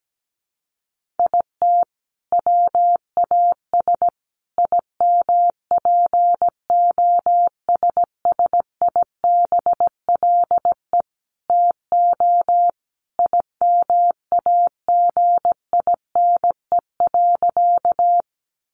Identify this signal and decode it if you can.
Morse code, heard as a keyed tone, about 17 wpm, 700 Hz. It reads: IT WAS IMPOSSIBLE TO IMAGINE.
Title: Sentences from Top 1400 Words 17wpm